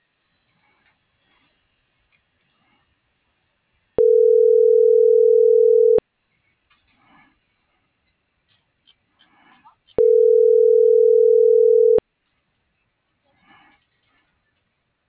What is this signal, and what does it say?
Ringback tone (North America), 440 + 480 Hz.
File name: CENTRAL MAYAB / out raulcristinomayab